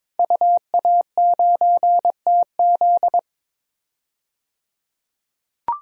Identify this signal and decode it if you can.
Morse code, heard as a keyed tone, about 22 wpm, 700 Hz. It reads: UA9TZ
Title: DXCC-100 Call Signs - 5 Characters - 22wpm